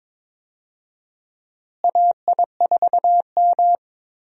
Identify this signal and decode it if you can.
Morse code, heard as a keyed tone, about 22 wpm, 700 Hz. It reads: AI4M
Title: USA Call Signs - 4 Characters - Repeated 3x - 22wpm